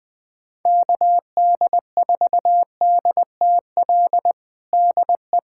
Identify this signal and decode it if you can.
Morse code, heard as a keyed tone, about 20 wpm, 700 Hz. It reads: KD4DTL DE
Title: USA Callsign DE Callsign - 6 Characters - 20wpm